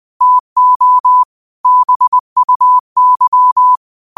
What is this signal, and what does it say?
Morse code, heard as a keyed tone, about 20 wpm, 1000 Hz. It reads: TO BUY